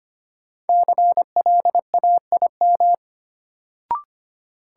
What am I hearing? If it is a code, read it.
Morse code, heard as a keyed tone, about 25 wpm, 700 Hz. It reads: CLAIM
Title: Sentences from Top 1500 Words - New Word Drill Alt - 25wpm